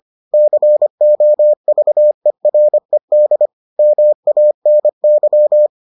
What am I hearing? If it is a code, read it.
Morse code, heard as a keyed tone, about 25 wpm, 600 Hz. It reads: COVERED MANY